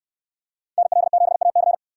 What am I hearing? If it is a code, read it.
Morse code, heard as a keyed tone, about 66 wpm, 700 Hz. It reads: NL7AC